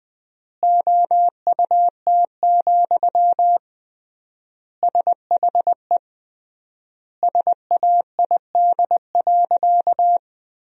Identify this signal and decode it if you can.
Morse code, heard as a keyed tone, about 20 wpm, 700 Hz. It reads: OUT, SHE SAID.
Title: 3x Word-Spacing - Sentences from Top 500 Words - 20wpm